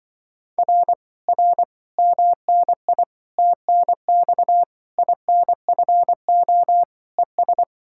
Morse code, heard as a keyed tone, about 24 wpm, 700 Hz. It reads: R R MNI TNX INFO ES
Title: Speed-Racing Common QSO Phrases - 20wpm